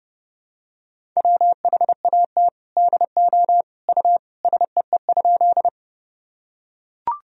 Morse code, heard as a keyed tone, about 30 wpm, 700 Hz. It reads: WHAT DO U SEE?
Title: Sentences from Top 100 Words and Abbreviations 30wpm